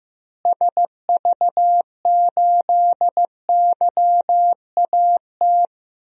Morse code, heard as a keyed tone, about 15 wpm, 700 Hz. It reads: SV8YAT
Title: DXCC-100 Call Signs - 6 Characters - Repeated 2x - 15wpm